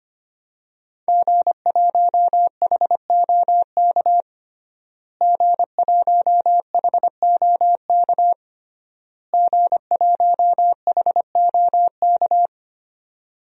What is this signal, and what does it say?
Morse code, heard as a keyed tone, about 25 wpm, 700 Hz. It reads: G1HOK G1HOK G1HOK
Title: DXCC-100 Call Signs - 5 Characters - Repeated 3x - 25wpm